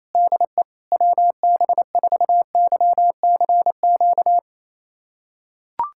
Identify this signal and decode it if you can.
Morse code, heard as a keyed tone, about 28 wpm, 700 Hz. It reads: DE WB4YCQ
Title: USA Callsign DE Callsign - 6 Characters - 28wpm